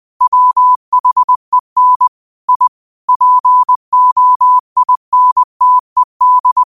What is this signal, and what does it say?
Morse code, heard as a keyed tone, about 20 wpm, 1000 Hz. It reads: WHEN I POINTED
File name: 2024-09-13-20wpm